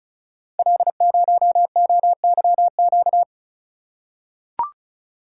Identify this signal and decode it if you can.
Morse code, heard as a keyed tone, about 36 wpm, 690 Hz. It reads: L0OYQ
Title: Code Groups - 5 Letter:Numbers - 35wpm